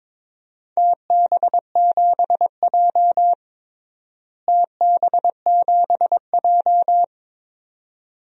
Morse code, heard as a keyed tone, about 22 wpm, 700 Hz. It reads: TB7J TB7J
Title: DXCC-100 Call Signs - 4 Characters - Repeated 2x - 22wpm